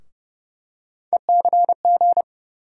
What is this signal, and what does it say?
Morse code, heard as a keyed tone, about 32 wpm, 700 Hz. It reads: ECG
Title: USA Call Sign Postfixes - 3 Letters 30wpm